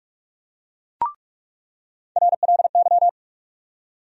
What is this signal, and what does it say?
Morse code, heard as a keyed tone, about 46 wpm, 700 Hz. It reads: RLY
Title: USA Call Sign Postfixes - 3 Letters 45wpm